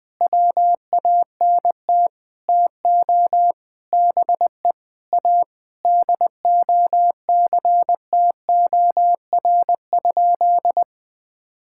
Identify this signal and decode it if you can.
Morse code, heard as a keyed tone, about 20 wpm, 700 Hz. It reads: WANT TO BE A DOCTOR?
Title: Sentences with 4-Word Phrases - 20wpm